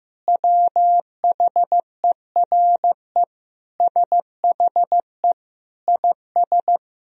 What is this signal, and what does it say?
Morse code, heard as a keyed tone, about 15 wpm, 700 Hz. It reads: WHERE SHE IS
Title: Sentences from Top 1000 Words - Review Part 2 - 15wpm